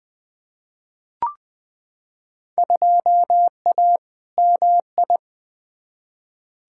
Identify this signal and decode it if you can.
Morse code, heard as a keyed tone, about 20 wpm, 700 Hz. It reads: 2A MI
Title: Field Day Exchanges - 20wpm